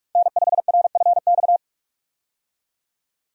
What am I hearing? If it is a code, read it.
Morse code, heard as a keyed tone, about 46 wpm, 700 Hz. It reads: N5RFX
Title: USA Call Signs - 5 Characters - 45wpm